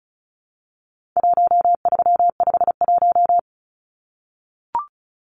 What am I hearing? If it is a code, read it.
Morse code, heard as a keyed tone, about 34 wpm, 700 Hz. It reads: 1351
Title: Numbers - 4-Digit 35wpm